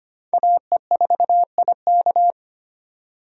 Morse code, heard as a keyed tone, about 25 wpm, 700 Hz. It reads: AE4IK
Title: USA Call Signs - 5 Characters - 25wpm